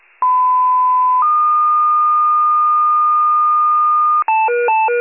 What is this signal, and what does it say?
Two-tone page, heard as a one-second tone, then a three-second tone, about 1010 then 1220 Hz. It is heard from 0.2 to 4.2 s.
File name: pager / 94 RESCUE / wcalert